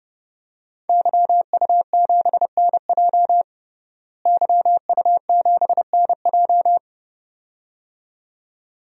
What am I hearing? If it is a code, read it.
Morse code, heard as a keyed tone, about 32 wpm, 700 Hz. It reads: YU7NJ YU7NJ
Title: DXCC-100 Call Signs - 5 Characters - Repeated 2x - 30wpm